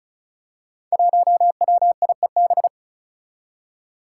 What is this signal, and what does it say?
Morse code, heard as a keyed tone, about 36 wpm, 700 Hz. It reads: 1WIEB
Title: Five Challenging Letter-Number Combinations 35wpm